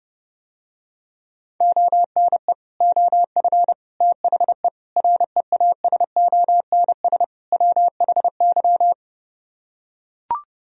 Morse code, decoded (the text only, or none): ONE OF THE REASONS WHY